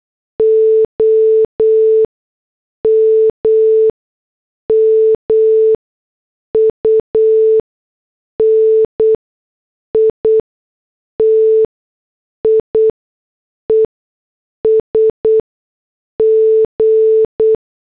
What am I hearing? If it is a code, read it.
Morse code, heard as a keyed tone, about 8 wpm, 440 Hz. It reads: O M M U N I T I E S G
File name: morse / headlines